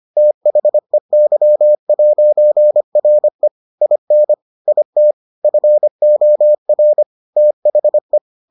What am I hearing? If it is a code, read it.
Morse code, heard as a keyed tone, about 25 wpm, 600 Hz. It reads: THEY'RE IN IT FOR THE